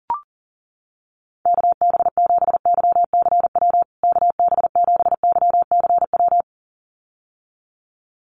Morse code, heard as a keyed tone, about 40 wpm, 700 Hz. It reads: KB7YCW KB7YCW
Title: USA Call Signs Repeated 2x 40wpm